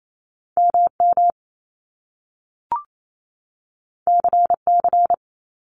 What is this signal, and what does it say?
Morse code, heard as a keyed tone, about 28 wpm, 700 Hz. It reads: MM CC
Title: Top Two Letter Combinations in Words 28wpm